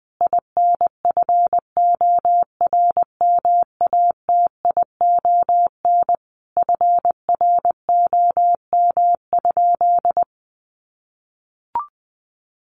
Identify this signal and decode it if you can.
Morse code, heard as a keyed tone, about 20 wpm, 700 Hz. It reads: INFORMATION FROM?
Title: Sentences from Top 600 Words 20wpm